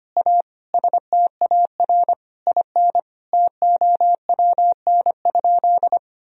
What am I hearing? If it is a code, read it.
Morse code, heard as a keyed tone, about 25 wpm, 700 Hz. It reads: A STAR IN TOWN?
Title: Sentences from Top 1000 Words - Review Part 2 - 25wpm